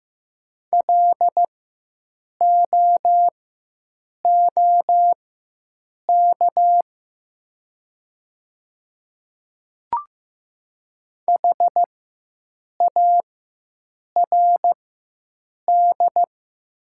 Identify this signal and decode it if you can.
Morse code, heard as a keyed tone, about 15 wpm, spaced as for 7 wpm, 700 Hz. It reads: LOOK HARD
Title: 4x Character Spacing - Top 300 Words - 15wpm